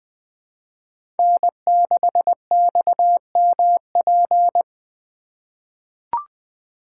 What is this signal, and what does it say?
Morse code, heard as a keyed tone, about 20 wpm, 700 Hz. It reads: N6XMP
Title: USA Call Signs - 5 Characters - 3x Character Spacing - 20wpm